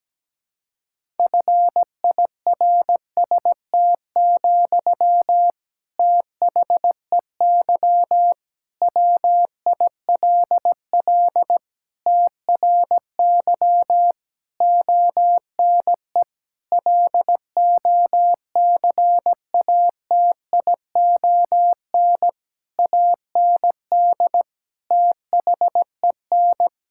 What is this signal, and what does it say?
Morse code, heard as a keyed tone, about 17 wpm, 700 Hz. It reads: FIRST, THEY WILL TRY ONE LOCATION AND THEN
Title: Sentences from Top 1000 Words - Review Part 6 - 17wpm